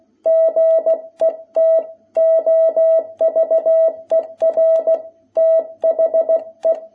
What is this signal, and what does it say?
Morse code, heard as a keyed tone, about 16 wpm, 660 Hz. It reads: GET OVER THE
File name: jh8cbh / 2021cw